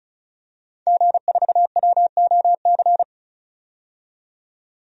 Morse code, heard as a keyed tone, about 36 wpm, 700 Hz. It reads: G4WOC
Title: DXCC-100 Call Signs - 5 Characters - 35wpm